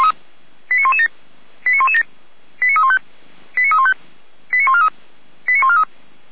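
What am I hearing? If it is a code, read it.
Five-tone selective call (ZVEI), heard as quick runs of five tones, tones of about 70 ms. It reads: 89117 89117 89315 89315 89143 89143
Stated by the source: Five-tone ZVEI selective call received on an SDR by feuerwehr-orlamuende (fire brigade Orlamünde).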